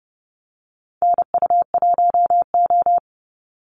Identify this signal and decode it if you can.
Morse code, heard as a keyed tone, about 30 wpm, 700 Hz. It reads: NU1O
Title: Call Signs Repeated 2x - 30wpm